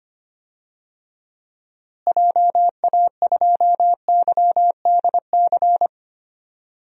Morse code, heard as a keyed tone, about 25 wpm, 700 Hz. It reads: JA2YDC
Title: DXCC-100 Call Signs - 6 Characters - Repeated 3x - 25wpm